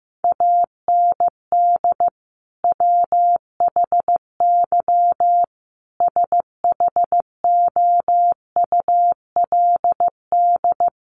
Morse code, heard as a keyed tone, about 15 wpm, 700 Hz. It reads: AND WHY SHOULD
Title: Sentences from Top 300 Words 15wpm